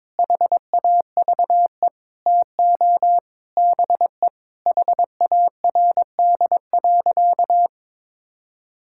Morse code, heard as a keyed tone, about 22 wpm, 700 Hz. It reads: HAVE TO BE HARD.
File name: practice / 22wpm / Sentences with 4-Word Phrases - 22wpm